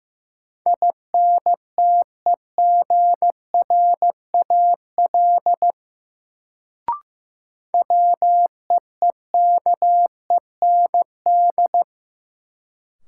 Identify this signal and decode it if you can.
Morse code, heard as a keyed tone, about 15 wpm, 700 Hz. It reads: INTEGRAL WEEKEND
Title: Sets of 5 Words - New Word Drill - 15wpm